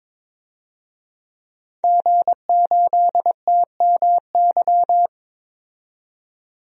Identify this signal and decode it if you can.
Morse code, heard as a keyed tone, about 21 wpm, 700 Hz. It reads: G8TMY
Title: DXCC-100 Call Signs - 5 Characters - 22wpm